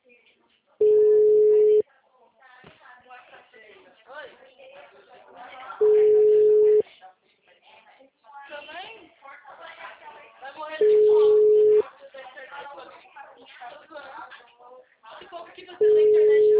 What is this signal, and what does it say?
Ringback tone (Europe, 425 Hz), with speech.